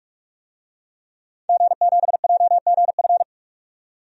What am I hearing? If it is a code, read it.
Morse code, heard as a keyed tone, about 42 wpm, 700 Hz. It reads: G7JGF
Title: DXCC-100 Call Signs - 5 Characters - 45wpm